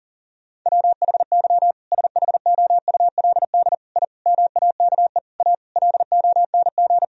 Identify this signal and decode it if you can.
Morse code, heard as a keyed tone, about 40 wpm, 700 Hz. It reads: WHY SHOULD I MAKE A LONG
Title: Sentences from Top 1000 Words - Review Part 1 - 40wpm